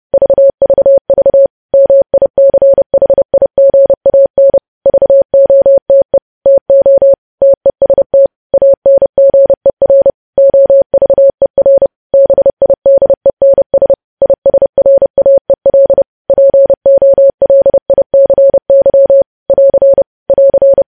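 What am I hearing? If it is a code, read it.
Morse code, heard as a keyed tone, about 30 wpm, 550 Hz. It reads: VVV MICHIGAN VOTE TO TEST ANGER OVER BIDENS ISRAEL POLICY <AR> <AR>
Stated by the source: Morse code news headlines sent at 30, preceded by vvv vvv.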